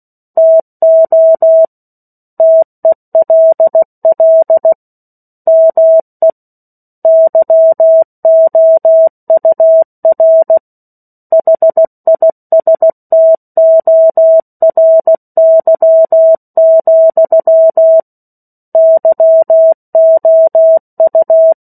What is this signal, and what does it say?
Morse code, heard as a keyed tone, about 16 wpm, 650 Hz. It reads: TO TELL ME YOUR HISTORY, YOU
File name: 03-Chapter-03 16 WPM